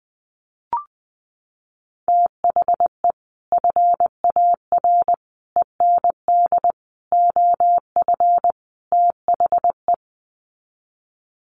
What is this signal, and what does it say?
Morse code, heard as a keyed tone, about 20 wpm, 700 Hz. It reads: THE FAR END OF THE
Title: Sets of 5 Words 20wpm